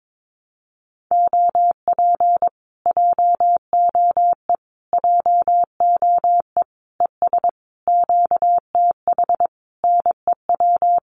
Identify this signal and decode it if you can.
Morse code, heard as a keyed tone, about 22 wpm, 700 Hz. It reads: OP JOE JOE ES QTH NEW
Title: Common QSO Phrases Encore 22wpm